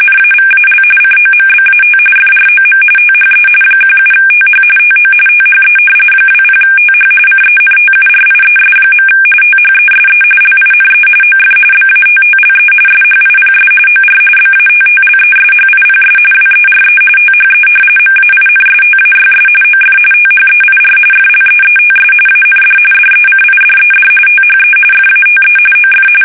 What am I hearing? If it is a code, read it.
RTTY, about 45.45 baud. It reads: NAVAL COMBAT FROM THE 17TH TO THE 20TH CENTURIES. THESE LINE OF BATTLE SHIPS OR BATTLESHIPS WERE THE MAINSTAYS OF THEIR FLEETS AND DECIDED THE FATE OF
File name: fsk850 / battleshipnj